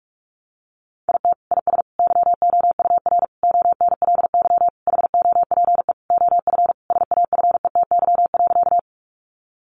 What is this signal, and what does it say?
Morse code, heard as a keyed tone, about 44 wpm, 700 Hz. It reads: IT IS YOUR ONLY HOPE OF SAFETY.
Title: Sentences from Top 1000 Words 45wpm